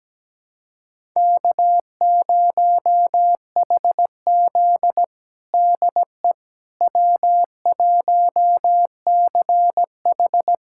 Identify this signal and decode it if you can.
Morse code, heard as a keyed tone, about 17 wpm, 700 Hz. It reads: K0HZ DE W1CH
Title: USA Callsign DE Callsign - 4 Characters - 17wpm